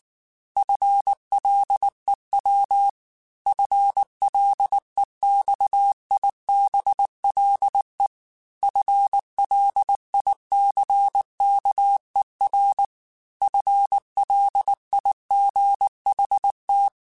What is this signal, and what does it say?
Morse code, heard as a keyed tone, about 19 wpm, 800 Hz. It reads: FLEW FLEXIBLE FLICKER FLIGHT